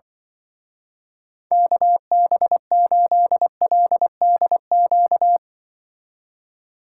Morse code, heard as a keyed tone, about 25 wpm, 700 Hz. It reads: KB8LDQ
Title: Speed Racing - USA Call Signs 20wpm